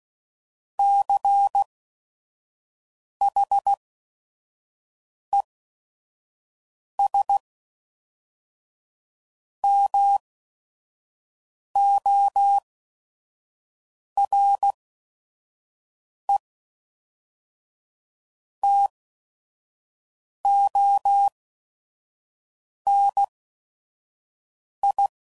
Morse code, heard as a keyed tone, about 16 wpm, spaced as for 5 wpm, 800 Hz. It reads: CHES MORE TO NI
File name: CW Code Groups @5WPM / QSOs / QSO WA7QOX 1